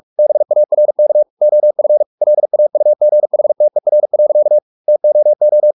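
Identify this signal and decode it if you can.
Morse code, heard as a keyed tone, about 44 wpm, 600 Hz. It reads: BARK OF LAUGHTER. TOO
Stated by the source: Morse practice at 45 wpm, the novel Never by Ken Follett in English, from CW med Gnister.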